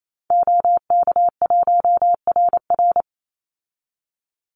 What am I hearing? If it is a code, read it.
Morse code, heard as a keyed tone, about 28 wpm, 700 Hz. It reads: OK1RR
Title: Call Signs Repeated 2x - 28wpm